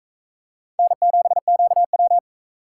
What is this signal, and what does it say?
Morse code, heard as a keyed tone, about 42 wpm, 700 Hz. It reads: N7QW